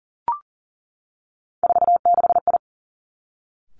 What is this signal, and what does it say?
Morse code, heard as a keyed tone, about 38 wpm, 700 Hz. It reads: 46I
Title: Three Challenging Letter-Number Combinations 40wpm